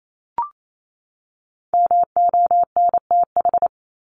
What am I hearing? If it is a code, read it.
Morse code, heard as a keyed tone, about 28 wpm, 700 Hz. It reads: MONTH
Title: ES:S Suffix - Top 100 Words with Base Words 28wpm